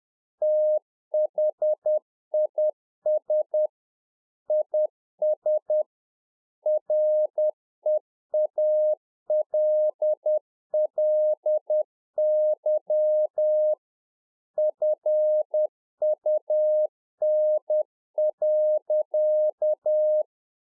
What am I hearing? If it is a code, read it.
Morse code, heard as a keyed tone, about 10 wpm, 610 Hz. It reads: THIS IS REALLY FUN.